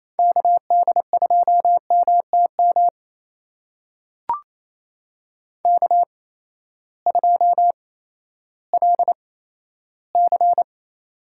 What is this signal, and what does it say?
Morse code, heard as a keyed tone, about 28 wpm, spaced as for 14 wpm, 700 Hz. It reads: KD2MTM K2LC
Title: USA Call Signs - 8x Character Spacing - 28wpm